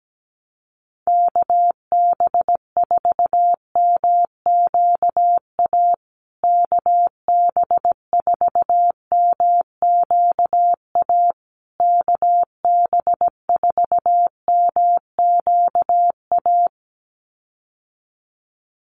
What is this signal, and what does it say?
Morse code, heard as a keyed tone, about 17 wpm, 700 Hz. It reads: KB4MQA KB4MQA KB4MQA
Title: USA Call Signs Repeated 3x 17wpm